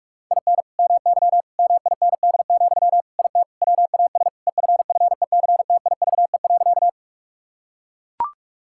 Morse code, heard as a keyed tone, about 42 wpm, 700 Hz. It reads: IN MY MIND, IT WAS EFFECTIVE.
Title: Sentences from Top 1000 Words - Review Part 6 - 45wpm